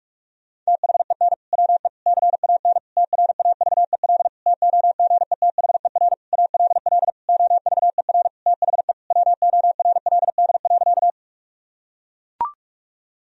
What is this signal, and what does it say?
Morse code, heard as a keyed tone, about 41 wpm, 700 Hz. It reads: THEN WE CAN TRAVEL TOGETHER ALL OVER THE WORLD.
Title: Sentences from Top 1000 Words - Review Part 2 - 45wpm